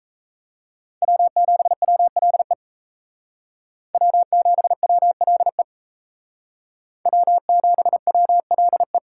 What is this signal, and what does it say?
Morse code, heard as a keyed tone, about 34 wpm, 700 Hz. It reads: W7WLE W7WLE W7WLE
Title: Speed Racing - USA 1x3 Call Signs 28wpm